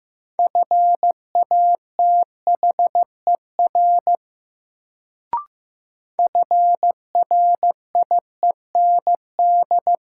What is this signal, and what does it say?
Morse code, heard as a keyed tone, about 15 wpm, 700 Hz. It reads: FATHER FRIEND
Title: Sentences from Top 300 Words - New Word Drill - 15wpm